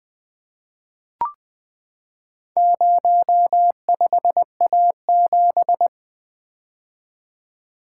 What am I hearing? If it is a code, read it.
Morse code, heard as a keyed tone, about 20 wpm, 700 Hz. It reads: 05A7
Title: Four Letter-Number Combinations - 20wpm